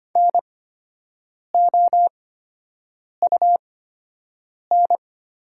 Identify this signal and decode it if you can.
Morse code, heard as a keyed tone, about 25 wpm, spaced as for 7 wpm, 690 Hz. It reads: NOUN
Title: 8x Character Spacing - Top 500 Words - 25wpm